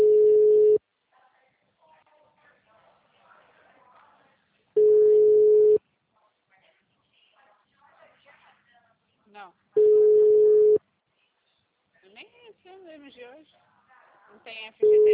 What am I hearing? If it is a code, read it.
Ringback tone (Europe, 425 Hz), with speech.
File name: out- -2044- - - .51439